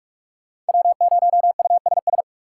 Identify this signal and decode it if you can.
Morse code, heard as a keyed tone, about 58 wpm, spaced as for 51 wpm, 700 Hz. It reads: W0USS
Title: Speed Racing - USA 1x3 Call Signs 45wpm